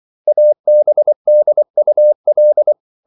Morse code, heard as a keyed tone, about 24 wpm, 600 Hz. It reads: ABDUL